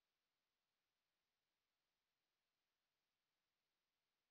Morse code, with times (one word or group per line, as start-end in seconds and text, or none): none